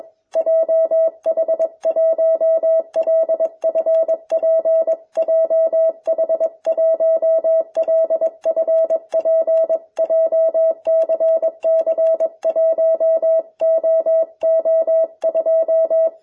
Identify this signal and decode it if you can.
Morse code, heard as a keyed tone, about 22 wpm, 660 Hz. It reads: JH1LFP JH1LFP JCC1OO2